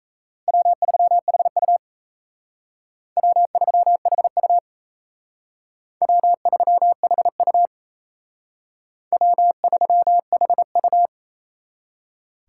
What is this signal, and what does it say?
Morse code, heard as a keyed tone, about 37 wpm, 700 Hz. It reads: W3HU W3HU W3HU W3HU